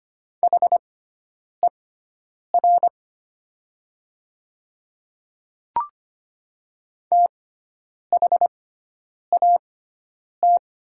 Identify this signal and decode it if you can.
Morse code, heard as a keyed tone, about 25 wpm, spaced as for 9 wpm, 700 Hz. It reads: HER THAT
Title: 6x Character Spacing - Top 100 Words - 25wpm